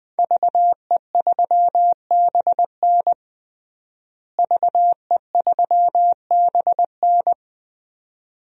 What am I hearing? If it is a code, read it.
Morse code, heard as a keyed tone, about 20 wpm, 700 Hz. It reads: VE3BN VE3BN
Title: DXCC-100 Call Signs - 5 Characters - Repeated 2x - 20wpm